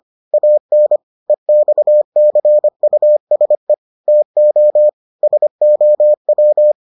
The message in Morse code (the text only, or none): AN EXCUSE TO SOW